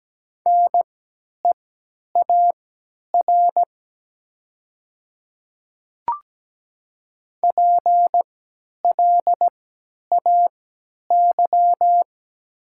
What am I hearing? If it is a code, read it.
Morse code, heard as a keyed tone, about 17 wpm, spaced as for 10 wpm, 700 Hz. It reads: NEAR PLAY
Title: 3x Character Spacing - Top 200 Words - 17wpm